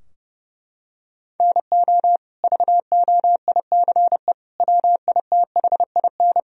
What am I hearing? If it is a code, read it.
Morse code, heard as a keyed tone, about 30 wpm, 700 Hz. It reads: NO VOICE WITHIN
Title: Sentences from Top 1000 Words - Review Part 4 - 30wpm